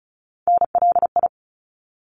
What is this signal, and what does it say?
Morse code, heard as a keyed tone, about 34 wpm, 700 Hz. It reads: NLI
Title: ARRL Sections 35wpm